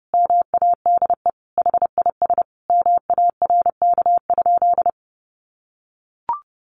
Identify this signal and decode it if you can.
Morse code, heard as a keyed tone, about 30 wpm, 700 Hz. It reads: MADE HIS MARK?
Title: Sentences from Top 300 Words 30wpm